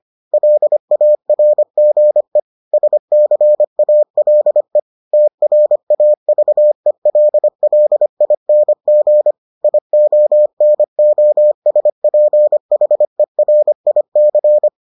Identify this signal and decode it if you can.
Morse code, heard as a keyed tone, about 25 wpm, 600 Hz. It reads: LARGE SCALE TRAVELLING IONOSPHERIC